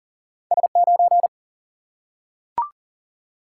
Morse code, heard as a keyed tone, about 40 wpm, 700 Hz. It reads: S 9
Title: Single-Letter Single-Number 40wpm